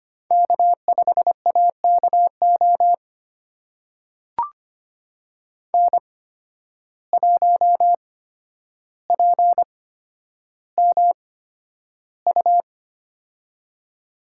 Morse code, heard as a keyed tone, about 25 wpm, spaced as for 11 wpm, 700 Hz. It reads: K5AKO N1PMU
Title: USA Call Signs - 5 Characters - 8x Character Spacing - 25wpm